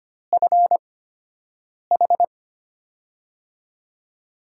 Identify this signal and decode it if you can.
Morse code, heard as a keyed tone, about 25 wpm, spaced as for 7 wpm, 700 Hz. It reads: FH
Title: USA Call Signs - 5 Characters - 8x Character Spacing - 25wpm